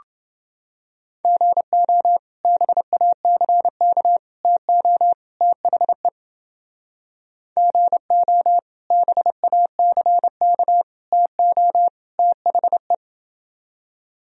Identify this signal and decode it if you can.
Morse code, heard as a keyed tone, about 29 wpm, 700 Hz. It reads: GO BACK TO THE GO BACK TO THE